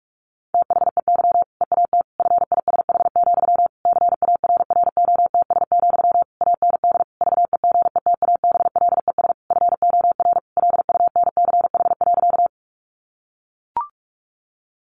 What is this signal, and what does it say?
Morse code, heard as a keyed tone, about 44 wpm, 700 Hz. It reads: THEY EAT FISH, CARROTS, AND VEGETABLES FOR LUNCH.
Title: Easy Phrases 45wpm